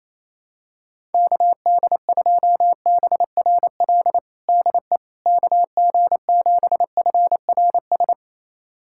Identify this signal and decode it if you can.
Morse code, heard as a keyed tone, about 28 wpm, 700 Hz. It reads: KD2BRL DE KG7FRS
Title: USA Callsign DE Callsign 28wpm